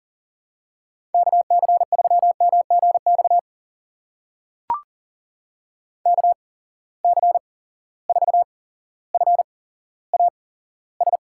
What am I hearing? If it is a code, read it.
Morse code, heard as a keyed tone, about 39 wpm, spaced as for 17 wpm, 700 Hz. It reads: KC3MGX KC4FAS
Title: USA Call Signs - 6 Characters - 8x Character Spacing - 40wpm